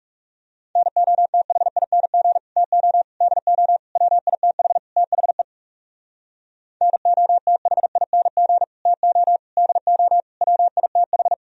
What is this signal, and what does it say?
Morse code, heard as a keyed tone, about 39 wpm, 700 Hz. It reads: NOTHING TO DO WITH THE NOTHING TO DO WITH